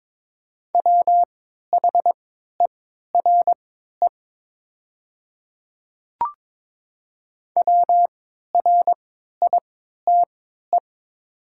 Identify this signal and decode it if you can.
Morse code, heard as a keyed tone, about 22 wpm, spaced as for 13 wpm, 700 Hz. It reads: WHERE WRITE
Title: 3x Character Spacing - Top 300 Words - 22wpm